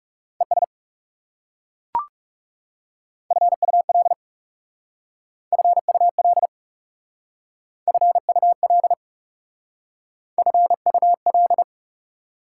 Morse code, heard as a keyed tone, about 36 wpm, 690 Hz. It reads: ES FUL FUL FUL FUL
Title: Speed-Racing Prefixes and Suffixes 30wpm